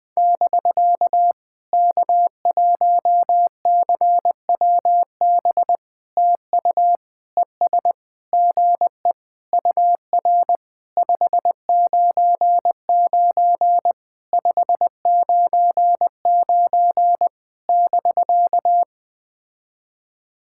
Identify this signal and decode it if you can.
Morse code, heard as a keyed tone, about 20 wpm, 700 Hz. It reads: <BK> K1CWB TU ES GE UR 599 599 <BK>
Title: POTA QSO Practice - 20wpm